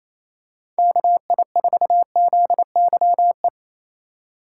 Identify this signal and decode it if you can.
Morse code, heard as a keyed tone, about 28 wpm, 700 Hz. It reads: KI4ZYE
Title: USA Call Signs - 6 Characters - 4x Character Spacing - 28wpm